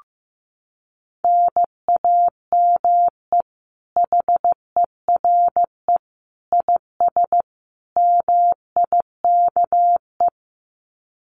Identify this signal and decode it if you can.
Morse code, heard as a keyed tone, about 15 wpm, 700 Hz. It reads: NAME HERE IS MIKE
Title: Common QSO Phrases 15wpm